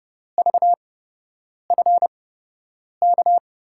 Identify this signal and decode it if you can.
Morse code, heard as a keyed tone, about 30 wpm, spaced as for 8 wpm, 700 Hz. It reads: VFK
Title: USA Call Signs - 6 Characters - 8x Character Spacing - 30wpm